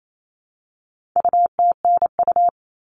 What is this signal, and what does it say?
Morse code, heard as a keyed tone, about 28 wpm, 700 Hz. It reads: UTNU
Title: All Cut Numbers - 4-Digit 28wpm